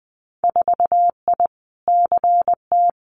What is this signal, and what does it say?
Morse code, heard as a keyed tone, about 20 wpm, 700 Hz. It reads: 4I CT
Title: Winter Field Day Exchanges Encore 20wpm